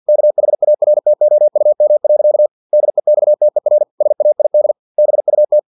Morse code, heard as a keyed tone, about 50 wpm, 600 Hz. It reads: KHARTOUM. DEXTER SAID BUT
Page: Never 49wpm